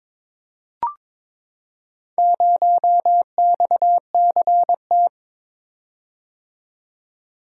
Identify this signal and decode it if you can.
Morse code, heard as a keyed tone, about 22 wpm, 700 Hz. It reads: 0XCT
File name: practice / 22wpm / Four Letter-Number Combinations - 22wpm